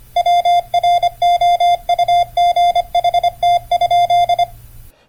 Morse code, heard as a keyed tone, about 25 wpm, 690 Hz. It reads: WROUGHT?